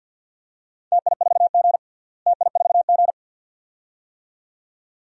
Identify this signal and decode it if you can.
Morse code, heard as a keyed tone, about 49 wpm, 700 Hz. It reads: TI4G TI4G
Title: DXCC-100 Call Signs - 4 Characters - Repeated 2x - 50wpm